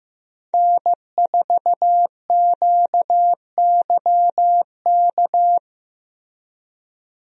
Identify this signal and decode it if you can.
Morse code, heard as a keyed tone, about 15 wpm, 700 Hz. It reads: N4QYK
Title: USA Call Signs - 5 Characters - Repeated 2x - 15wpm